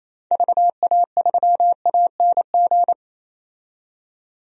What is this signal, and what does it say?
Morse code, heard as a keyed tone, about 28 wpm, 700 Hz. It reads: VA3ANG
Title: DXCC-100 Call Signs - 6 Characters - 28wpm